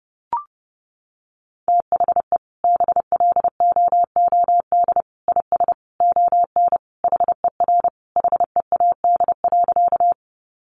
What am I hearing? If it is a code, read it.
Morse code, heard as a keyed tone, about 28 wpm, 700 Hz. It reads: THE BLOOD IS ON HER HEAD.
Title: Sentences from Top 1100 Words 30wpm